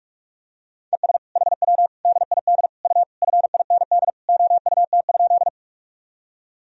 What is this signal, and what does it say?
Morse code, heard as a keyed tone, about 46 wpm, 700 Hz. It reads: ES HW DID U FIND OUT?
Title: Sentences from Top 100 Words and Abbreviations 45wpm